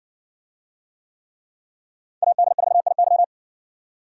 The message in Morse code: AD4IY

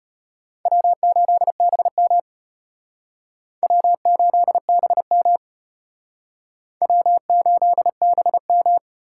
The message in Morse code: W8BM W8BM W8BM

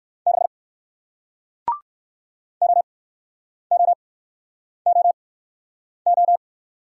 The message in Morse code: 6 OOOO

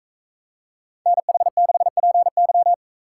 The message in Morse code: NH6PY

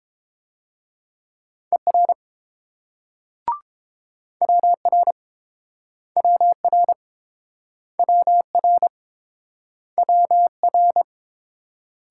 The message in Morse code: ER WR WR WR WR

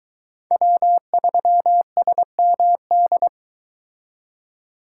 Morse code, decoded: W3SMD

